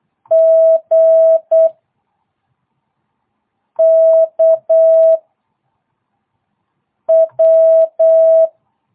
Morse code, heard as a keyed tone, about 8 wpm, 660 Hz. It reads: G K W